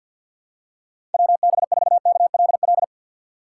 Morse code, heard as a keyed tone, about 49 wpm, 700 Hz. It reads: WB4KLL
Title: USA Call Signs - 6 Characters - 4x Character Spacing - 50wpm